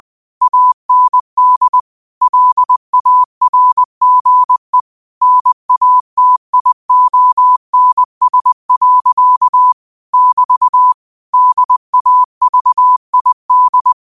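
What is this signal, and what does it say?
Morse code, heard as a keyed tone, about 20 wpm, 1000 Hz. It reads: AND LARGE NATIONS. <BT> DAVID
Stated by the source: Morse code quote of the day at 20 wpm.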